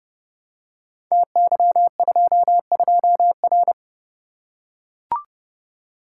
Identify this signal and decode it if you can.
Morse code, heard as a keyed tone, about 30 wpm, 700 Hz. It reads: TY22R
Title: Code Groups - 5 Letter:Numbers - 30wpm